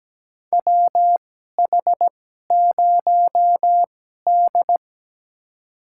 Morse code, heard as a keyed tone, about 17 wpm, spaced as for 12 wpm, 700 Hz. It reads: WH0D
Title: USA Call Signs - 4 Characters - 2x Character Spacing - 17wpm